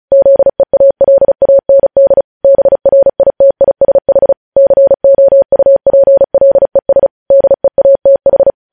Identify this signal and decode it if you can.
Morse code, heard as a keyed tone, about 36 wpm, 550 Hz. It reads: ZEALAND BRITISH COUPLES DEATH
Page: News Headlines in Morse Code at 35 WPM